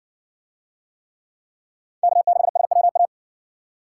Morse code, heard as a keyed tone, about 58 wpm, 700 Hz. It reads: K6SWA